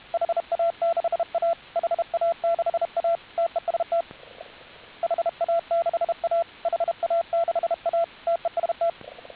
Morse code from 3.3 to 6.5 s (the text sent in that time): TEST HA6A